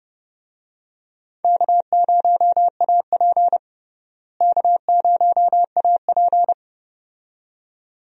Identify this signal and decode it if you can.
Morse code, heard as a keyed tone, about 30 wpm, 700 Hz. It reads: K0AP K0AP
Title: USA Call Signs - 4 Characters - Repeated 2x - 30wpm